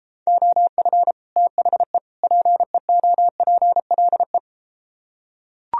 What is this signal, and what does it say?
Morse code, heard as a keyed tone, about 33 wpm, 700 Hz. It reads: OF THE PEOPLE